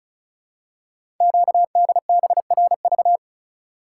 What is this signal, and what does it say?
Morse code, heard as a keyed tone, about 36 wpm, 700 Hz. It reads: QDBRV
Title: Code Groups - 5 Letters - 35wpm